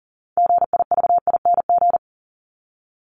Morse code, heard as a keyed tone, about 37 wpm, 700 Hz. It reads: GIVING
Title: ING Suffix - Top 100 Words with Base Words 40wpm